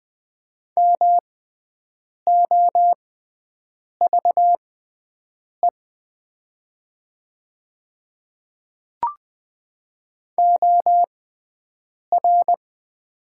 Morse code, heard as a keyed tone, about 20 wpm, spaced as for 7 wpm, 700 Hz. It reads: MOVE OR
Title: 6x Character Spacing - Top 200 Words - 20wpm